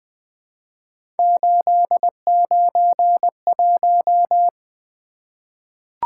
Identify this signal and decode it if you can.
Morse code, heard as a keyed tone, about 20 wpm, 700 Hz. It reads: 891
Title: Three Letter-Number Combinations - 20wpm